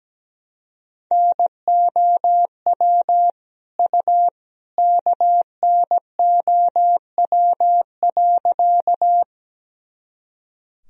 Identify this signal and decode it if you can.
Morse code, heard as a keyed tone, about 17 wpm, 700 Hz. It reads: NOW U KNOW.
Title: Sentences from Top 100 Words and Abbreviations 17wpm